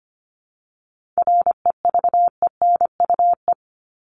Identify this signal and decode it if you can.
Morse code, heard as a keyed tone, about 25 wpm, 700 Hz. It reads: REVENUE
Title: Top 1501-1750 Words 25wpm